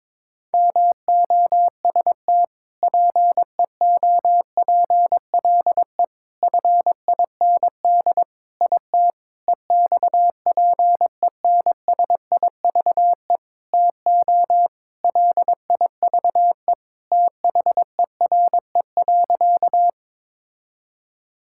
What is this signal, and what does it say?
Morse code, heard as a keyed tone, about 22 wpm, 700 Hz. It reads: MOST PEOPLE FIND IT EXPENSIVE TO LIVE THERE.
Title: Sentences from Top 1000 Words - Review Part 6 - 22wpm